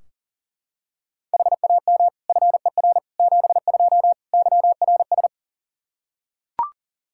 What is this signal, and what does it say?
Morse code, heard as a keyed tone, about 40 wpm, 700 Hz. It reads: HAM FER 72 YRS
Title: Practice QSO - Exchange 3 - More Info - 40wpm